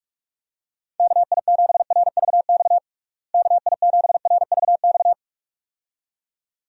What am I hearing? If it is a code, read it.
Morse code, heard as a keyed tone, about 46 wpm, 700 Hz. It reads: KI7RVX KI7RVX
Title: USA Call Signs - 6 Characters - Repeated 2x - 45wpm